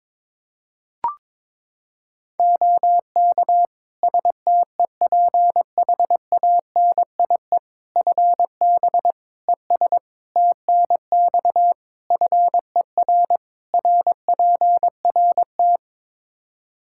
Morse code, heard as a keyed tone, about 21 wpm, 700 Hz. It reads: OK STEPHANIE FB ES TNX FER RPRT
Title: Practice QSO - Exchange 2 - Info - 22wpm